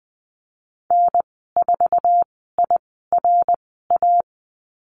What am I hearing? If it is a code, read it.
Morse code, heard as a keyed tone, about 20 wpm, spaced as for 15 wpm, 700 Hz. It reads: N4IRA
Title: 055 Lesson 10g - Letter r - Call Signs 20wpm